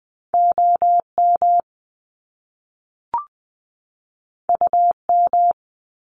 Morse code, heard as a keyed tone, about 20 wpm, 700 Hz. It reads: OM UM
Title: Top Two Letter Combinations in Words 20wpm